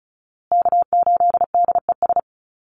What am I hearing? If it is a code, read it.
Morse code, heard as a keyed tone, about 34 wpm, 700 Hz. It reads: K8DES
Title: USA Call Signs Repeated 3x 35wpm